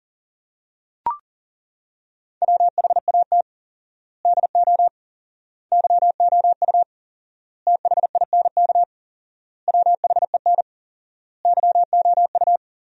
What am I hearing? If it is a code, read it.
Morse code, heard as a keyed tone, about 40 wpm, 700 Hz. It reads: WHAT DO YOU THINK WHEN YOU
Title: 4x Word-Spacing - Sentences from Top 300 Words - 40wpm